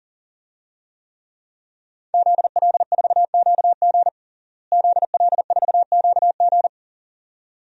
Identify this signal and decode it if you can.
Morse code, heard as a keyed tone, about 40 wpm, 700 Hz. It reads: ZL4QG ZL4QG
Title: DXCC-100 Call Signs - 5 Characters - Repeated 2x - 40wpm